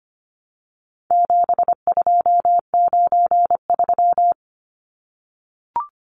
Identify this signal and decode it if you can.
Morse code, heard as a keyed tone, about 25 wpm, 700 Hz. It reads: 7293
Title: Numbers - 4-Digit 25wpm